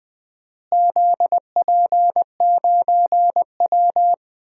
Morse code, heard as a keyed tone, about 20 wpm, 700 Hz. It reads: ZP9W
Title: DXCC-100 Call Signs - 4 Characters - Repeated 3x - 20wpm